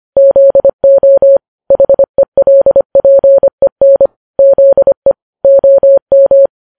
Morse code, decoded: ZO HELPEN ZE OM